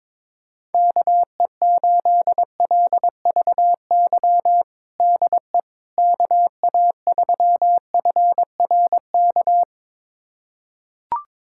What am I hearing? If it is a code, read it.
Morse code, heard as a keyed tone, about 22 wpm, 700 Hz. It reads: KE8LVY DE KA3FRK
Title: USA Callsign DE Callsign - 6 Characters - 22wpm